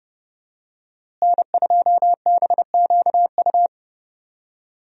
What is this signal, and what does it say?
Morse code, heard as a keyed tone, about 30 wpm, 700 Hz. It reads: N2BQU
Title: USA Call Signs - 5 Characters - Repeated 3x - 30wpm